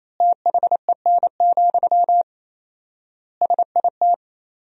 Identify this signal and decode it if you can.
Morse code, heard as a keyed tone, about 28 wpm, 700 Hz. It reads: THEN, SIT
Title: 4x Word-Spacing - Sentences from Top 500 Words - 28wpm